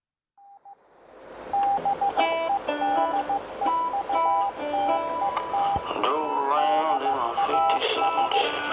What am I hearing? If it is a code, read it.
Morse code, heard as a keyed tone, about 14 wpm, 800 Hz, with speech and radio noise. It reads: BEDEUTET DAS